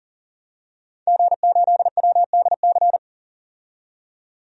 Morse code, heard as a keyed tone, about 40 wpm, 700 Hz. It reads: G8WDC
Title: DXCC-100 Call Signs - 5 Characters - 40wpm